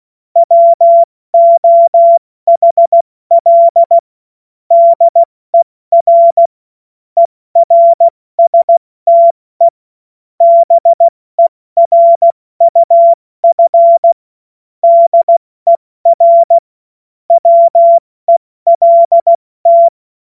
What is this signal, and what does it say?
Morse code, heard as a keyed tone, about 16 wpm, 680 Hz. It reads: WOHL DER ERSTE BERUF DER WELT
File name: Morsekurs / Witze / Witz 21.November 2023 70 F80